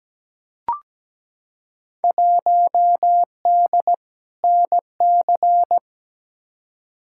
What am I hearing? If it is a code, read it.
Morse code, heard as a keyed tone, about 17 wpm, 700 Hz. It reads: 1D NC
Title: Field Day Exchanges Encore - 17wpm